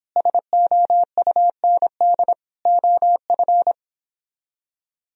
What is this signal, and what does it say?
Morse code, heard as a keyed tone, about 29 wpm, 700 Hz. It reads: SOUND OF